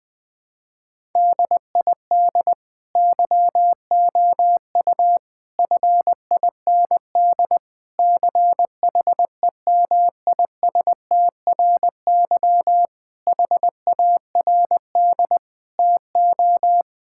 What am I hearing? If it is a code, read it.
Morse code, heard as a keyed tone, about 20 wpm, 700 Hz. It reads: DID YOU FIND CHEMISTRY HARD TO
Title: Sentences from Top 1000 Words - Review Part 6 - 20wpm